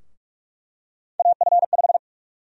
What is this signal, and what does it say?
Morse code, heard as a keyed tone, about 46 wpm, 700 Hz. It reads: AF5
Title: USA Call Sign Prefixes - Repeated 2x - 45wpm